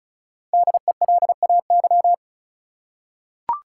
Morse code, heard as a keyed tone, about 34 wpm, 700 Hz. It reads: DELAY
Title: Five Letter Words - 35wpm